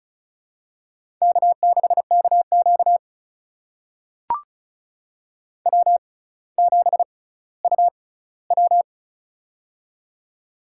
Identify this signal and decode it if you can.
Morse code, heard as a keyed tone, about 36 wpm, spaced as for 18 wpm, 700 Hz. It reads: K6KQ W7UW
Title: USA Call Signs - 4 Characters - 6x Character Spacing - 35wpm